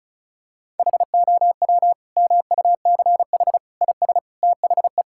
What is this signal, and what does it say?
Morse code, heard as a keyed tone, about 36 wpm, 690 Hz. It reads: HOW MUCH IS THE
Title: Sentences from Top 1000 Words - Review Part 6 - 35wpm